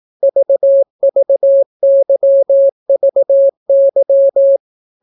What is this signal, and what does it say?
Morse code, heard as a keyed tone, about 18 wpm, 550 Hz. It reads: VVYVY